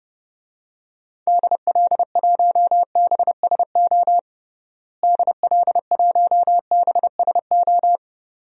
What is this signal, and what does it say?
Morse code, heard as a keyed tone, about 32 wpm, 700 Hz. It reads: DL1BSO DL1BSO
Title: DXCC-100 Call Signs - 6 Characters - Repeated 2x - 30wpm